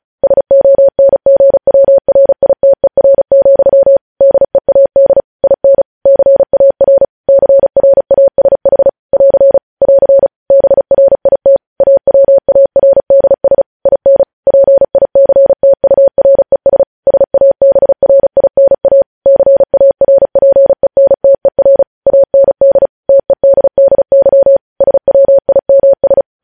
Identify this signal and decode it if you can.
Morse code, heard as a keyed tone, about 36 wpm, 550 Hz. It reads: SONGWRITER, DEAD IN CAR CRASH <AR> <AR> BRIT AWARDS IN PICTURES SABRINA CARPENTER AND TEDDY SWIMS